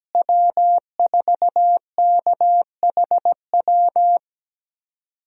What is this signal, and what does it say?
Morse code, heard as a keyed tone, about 17 wpm, 700 Hz. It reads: W4KHW
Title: USA Call Signs - 5 Characters - Repeated 2x - 17wpm